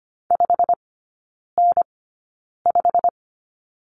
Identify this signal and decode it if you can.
Morse code, heard as a keyed tone, about 25 wpm, spaced as for 9 wpm, 690 Hz. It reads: 5N5
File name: Lesson 15e - Number 5 - Three Characters 25wpm